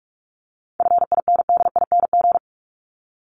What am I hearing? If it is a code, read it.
Morse code, heard as a keyed tone, about 44 wpm, 700 Hz. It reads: FINDING
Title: ING Suffix - Top 100 Words with Base Words 45wpm